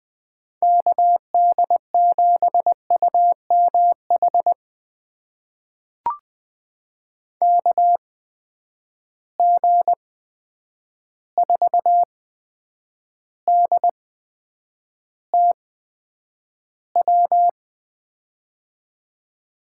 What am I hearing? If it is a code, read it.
Morse code, heard as a keyed tone, about 20 wpm, spaced as for 9 wpm, 700 Hz. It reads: KD7UMH KG4DTW
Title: USA Call Signs - 6 Characters - 8x Character Spacing - 20wpm